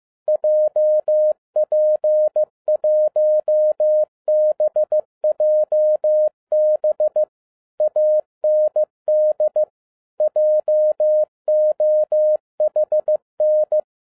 Morse code, heard as a keyed tone, about 15 wpm, 610 Hz. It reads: JP1BJB AND JOHN
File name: 2012-01-mp3 / n17-SUMMARY OF FEA NET IN 2011-15wpm